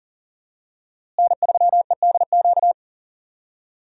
Morse code, heard as a keyed tone, about 40 wpm, 700 Hz. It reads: N3EDQ
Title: USA Call Signs - 5 Characters - Repeated 2x - 40wpm